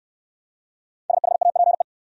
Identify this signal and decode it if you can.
Morse code, heard as a keyed tone, about 59 wpm, 700 Hz. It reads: SHAPE